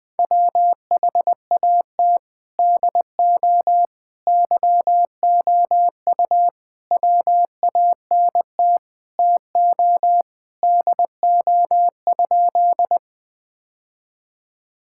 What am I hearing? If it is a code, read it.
Morse code, heard as a keyed tone, about 20 wpm, 700 Hz. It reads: WHAT DO YOU WANT TO DO?
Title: Sentences with 3-Word Phrases - Spoken Beginning - 20wpm